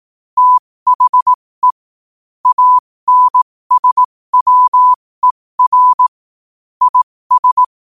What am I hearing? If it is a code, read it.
Morse code, heard as a keyed tone, about 18 wpm, spaced as for 15 wpm, 1000 Hz. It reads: THE ANSWER IS
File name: qotdfiles / 15wpm